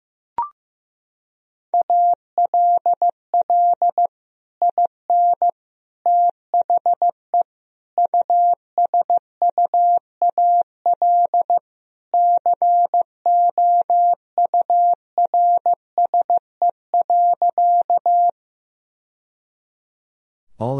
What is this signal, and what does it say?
Morse code, heard as a keyed tone, about 15 wpm, 700 Hz. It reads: ALL IN THE USUAL COURSE.
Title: Sentences from Top 1000 Words - Review Part 2 - 15wpm